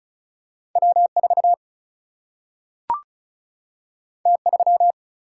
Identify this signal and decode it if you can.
Morse code, heard as a keyed tone, about 36 wpm, 700 Hz. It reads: W 4 T 3
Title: Single-Letter Single-Number 35wpm